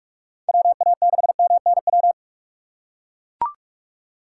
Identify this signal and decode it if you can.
Morse code, heard as a keyed tone, about 44 wpm, 700 Hz. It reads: WA6MNW
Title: USA Call Signs - 6 Characters - 6x Character Spacing - 45wpm